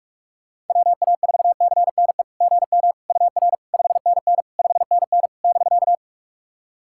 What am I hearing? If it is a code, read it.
Morse code, heard as a keyed tone, about 48 wpm, 700 Hz. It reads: WA4CNE GM UR 5NN 5NN <BK>
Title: SOTA QSO Practice 45wpm